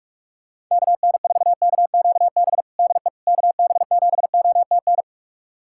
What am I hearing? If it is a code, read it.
Morse code, heard as a keyed tone, about 46 wpm, 690 Hz. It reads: KN4KQB DE KB7OTN
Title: USA Callsign DE Callsign 45wpm